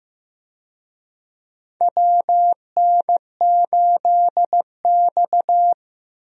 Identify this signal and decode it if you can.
Morse code, heard as a keyed tone, about 15 wpm, 700 Hz. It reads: WN8X
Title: USA Call Signs - 4 Characters - Repeated 3x - 15wpm